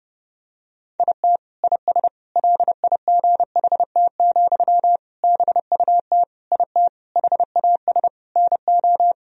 Morse code, heard as a keyed tone, about 32 wpm, 700 Hz. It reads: IT IS LIGHT, BUT IT HAS NO
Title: Sentences from Top 1000 Words - Review Part 1 - 30wpm